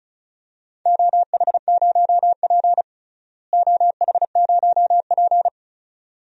Morse code, heard as a keyed tone, about 34 wpm, 700 Hz. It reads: OH0P OH0P
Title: DXCC-100 Call Signs - 4 Characters - Repeated 2x - 35wpm